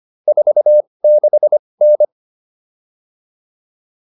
Morse code, heard as a keyed tone, about 25 wpm, spaced as for 20 wpm, 600 Hz. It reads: 46N